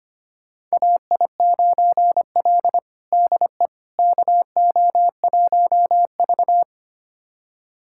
Morse code, heard as a keyed tone, about 25 wpm, 700 Hz. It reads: AI9L DE KO1V
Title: USA Callsign DE Callsign - 4 Characters - 25wpm